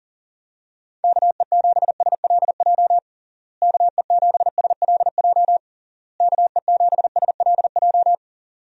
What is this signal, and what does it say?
Morse code, heard as a keyed tone, about 40 wpm, 700 Hz. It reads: KE7SLJ KE7SLJ KE7SLJ
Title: USA Call Signs - 6 Characters - Repeated 3x - 40wpm